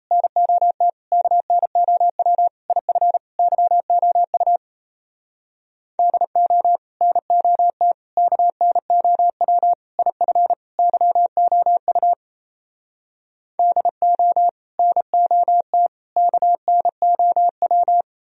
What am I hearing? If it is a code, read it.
Morse code, heard as a keyed tone, about 38 wpm, 700 Hz. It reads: NOT KNOW IF YOU DO NOT KNOW IF YOU DO NOT KNOW